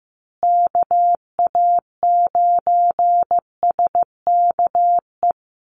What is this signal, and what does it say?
Morse code, heard as a keyed tone, about 15 wpm, 700 Hz. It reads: KA9SKE
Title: USA Call Signs Repeated 2x 15wpm